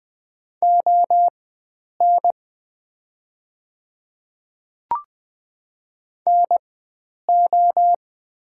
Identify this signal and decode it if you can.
Morse code, heard as a keyed tone, about 20 wpm, spaced as for 9 wpm, 700 Hz. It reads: ON NO
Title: 4x Character Spacing - Top 100 Words - 20wpm